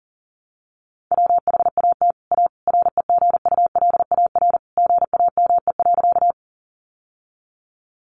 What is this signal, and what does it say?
Morse code, heard as a keyed tone, about 40 wpm, 700 Hz. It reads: WHAT A REGULAR GAME.
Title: Sentences from Top 900 Words 40wpm